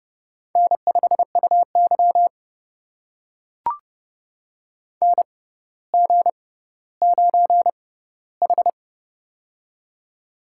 Morse code, heard as a keyed tone, about 30 wpm, spaced as for 15 wpm, 700 Hz. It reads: N5UY NG9H
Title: USA Call Signs - 4 Characters - 6x Character Spacing - 30wpm